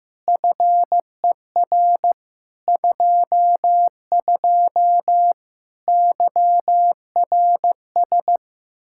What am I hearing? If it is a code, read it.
Morse code, heard as a keyed tone, about 15 wpm, 700 Hz. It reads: FER 22 YRS
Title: Practice QSO - Exchange 3 - More Info - 15wpm